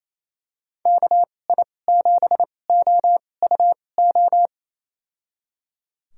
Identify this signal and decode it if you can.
Morse code, heard as a keyed tone, about 28 wpm, spaced as for 20 wpm, 700 Hz. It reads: KI7OUO
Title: USA Call Signs - 6 Characters - 2x Character Spacing - 28wpm